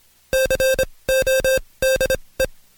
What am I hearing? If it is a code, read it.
Morse code, heard as a keyed tone, about 26 wpm, spaced as for 20 wpm, 530 Hz. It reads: CODE
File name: Code Quick / AudioDownld2 / Seg136-Stepper9 19WPM